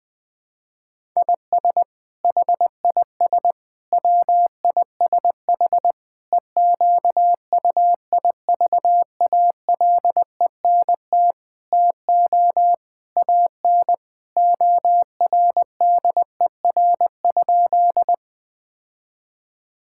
Morse code, heard as a keyed tone, about 21 wpm, 700 Hz. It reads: IS HIS WISH EQUIVALENT TO AN ORDER?
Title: Sentences from Top 1600 Words 20wpm